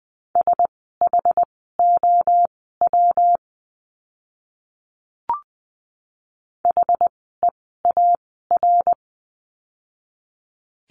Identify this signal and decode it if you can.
Morse code, heard as a keyed tone, about 20 wpm, spaced as for 14 wpm, 700 Hz. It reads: SHOW HEAR
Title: 115 Lesson 19f - Letter w - Words 20wpm